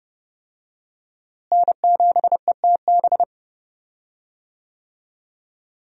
Morse code, heard as a keyed tone, about 31 wpm, 700 Hz. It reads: N7ETB
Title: USA Call Signs - 5 Characters - Repeated 2x - 30wpm